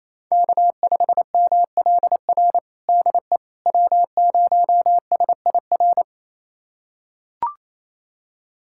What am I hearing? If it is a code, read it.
Morse code, heard as a keyed tone, about 28 wpm, 700 Hz. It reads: K5MLR DE W0SIR
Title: USA Callsign DE Callsign - 5 Characters - 28wpm